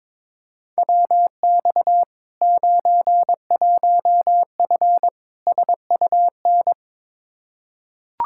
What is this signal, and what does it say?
Morse code, heard as a keyed tone, about 22 wpm, 700 Hz. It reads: WX 91F SUN
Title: Practice QSO - Exchange 2 - Info - 22wpm